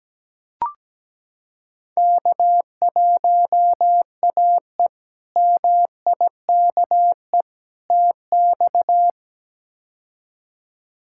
Morse code, heard as a keyed tone, about 17 wpm, 700 Hz. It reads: K1AE MIKE TX
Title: K1USN SST QSO Practice - 17wpm